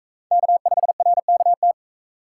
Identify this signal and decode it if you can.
Morse code, heard as a keyed tone, about 42 wpm, 700 Hz. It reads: K5RKT